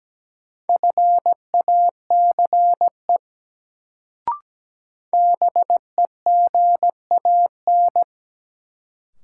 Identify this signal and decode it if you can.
Morse code, heard as a keyed tone, about 17 wpm, 700 Hz. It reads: FACE BEGAN
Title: Sentences from Top 300 Words - New Word Drill - 17wpm